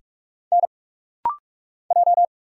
Morse code, heard as a keyed tone, about 46 wpm, 700 Hz. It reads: N J
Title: Single Letters - Rapid-Fire 45wpm